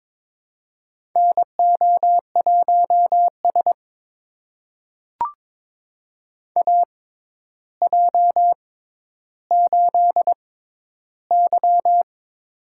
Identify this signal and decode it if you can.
Morse code, heard as a keyed tone, about 22 wpm, spaced as for 11 wpm, 700 Hz. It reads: NO1S AJ8Y
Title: USA Call Signs - 4 Characters - 6x Character Spacing - 22wpm